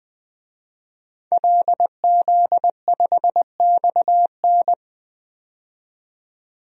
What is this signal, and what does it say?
Morse code, heard as a keyed tone, about 20 wpm, 700 Hz. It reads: LZ5XN
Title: DXCC-100 Call Signs - 5 Characters - 20wpm